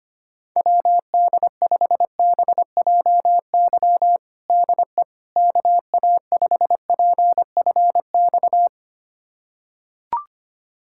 Morse code, heard as a keyed tone, about 25 wpm, 700 Hz. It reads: WD5BJY DE KA5PFX
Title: USA Callsign DE Callsign - 6 Characters - 25wpm